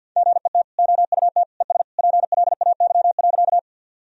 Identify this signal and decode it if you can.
Morse code, heard as a keyed tone, about 50 wpm, 700 Hz. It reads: GET OUT ES PLAY.
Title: Sentences from Top 200 Words and Abbreviations 50wpm